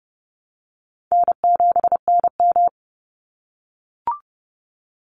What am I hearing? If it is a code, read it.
Morse code, heard as a keyed tone, about 30 wpm, 700 Hz. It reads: N7NM
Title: Call Signs - 30wpm